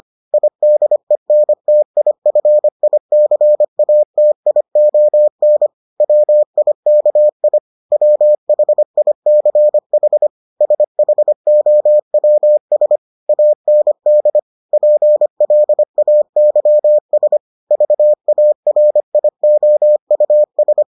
Morse code, heard as a keyed tone, about 25 wpm, 600 Hz. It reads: IDENTIFICATION WIKI WHICH SHOWS AND PLAYS VARIOUS